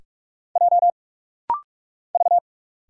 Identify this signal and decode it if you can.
Morse code, heard as a keyed tone, about 46 wpm, 700 Hz. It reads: J V
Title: Single Letters - Rapid-Fire 45wpm